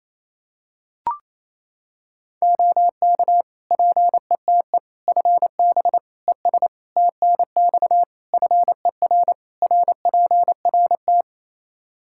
Morse code, heard as a keyed tone, about 28 wpm, 700 Hz. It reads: OK PETE FB ES TNX FER RPRT
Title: Practice QSO - Exchange 2 - Info - 28wpm